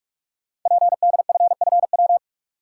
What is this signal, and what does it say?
Morse code, heard as a keyed tone, about 44 wpm, 700 Hz. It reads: PDFFW
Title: Code Groups - 5 Letter:Numbers - 45wpm